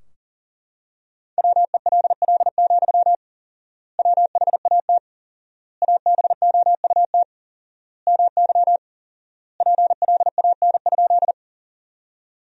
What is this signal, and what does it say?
Morse code, heard as a keyed tone, about 40 wpm, 700 Hz. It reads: WELL, WHAT ABOUT MY PLAN?
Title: 4x Word-Spacing - Sentences from Top 500 Words - 40wpm